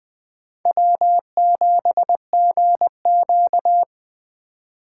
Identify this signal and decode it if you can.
Morse code, heard as a keyed tone, about 21 wpm, 700 Hz. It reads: W7GQ